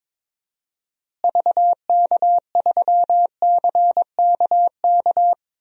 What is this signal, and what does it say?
Morse code, heard as a keyed tone, about 22 wpm, 700 Hz. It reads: VK3CKK
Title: DXCC-100 Call Signs - 6 Characters - Repeated 3x - 22wpm